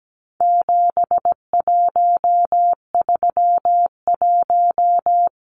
Morse code, heard as a keyed tone, about 17 wpm, 700 Hz. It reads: 7131
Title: Numbers - 4-Digit 17wpm